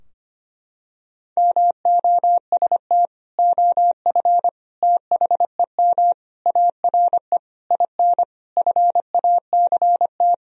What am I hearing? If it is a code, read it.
Morse code, heard as a keyed tone, about 25 wpm, 700 Hz. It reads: MOST OF THEM ARE IN FACT
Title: Sentences from Top 1000 Words - Review Part 4 - 25wpm